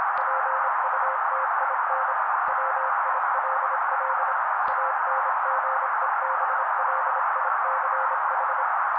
Morse code, heard as a keyed tone, about 27 wpm, 530 Hz. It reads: PUTIN WILL ANGEBLICH
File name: daily / 25wpm noise